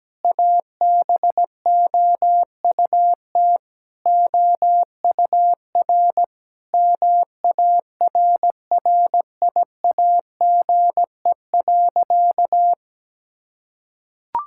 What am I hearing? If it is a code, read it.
Morse code, heard as a keyed tone, about 17 wpm, 700 Hz. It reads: ABOUT OUR MARRIAGE.
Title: Sentences from Top 1000 Words - Review Part 5 - 17wpm